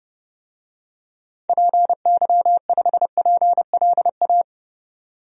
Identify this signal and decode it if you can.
Morse code, heard as a keyed tone, about 32 wpm, 700 Hz. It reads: PY5PLA
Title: DXCC-100 Call Signs - 6 Characters - 30wpm